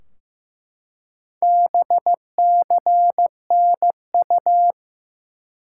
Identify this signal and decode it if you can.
Morse code, heard as a keyed tone, about 15 wpm, 700 Hz. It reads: BCNU
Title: Top 90 QSO Elements 15wpm